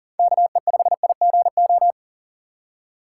Morse code, heard as a keyed tone, about 40 wpm, 700 Hz. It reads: KE5IGO
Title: USA Call Signs - 4x Character Spacing - 40wpm